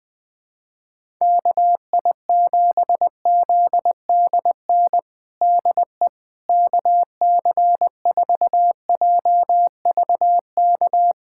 Morse code, heard as a keyed tone, about 20 wpm, 700 Hz. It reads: KI7ZDN DE KC4JVK
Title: USA Callsign DE Callsign - 6 Characters - 20wpm